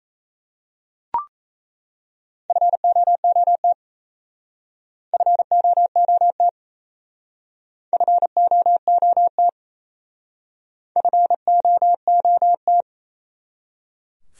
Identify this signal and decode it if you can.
Morse code, heard as a keyed tone, about 32 wpm, 700 Hz. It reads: FOOT FOOT FOOT FOOT
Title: Speed-Racing Top 500 Words 28wpm